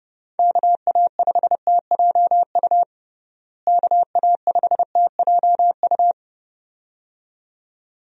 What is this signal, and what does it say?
Morse code, heard as a keyed tone, about 30 wpm, 700 Hz. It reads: KA5TJU KA5TJU
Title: USA Call Signs - 6 Characters - Repeated 2x - 30wpm